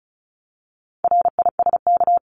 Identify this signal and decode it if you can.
Morse code, heard as a keyed tone, about 34 wpm, 700 Hz. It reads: RISK
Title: Top 501-750 Words 35wpm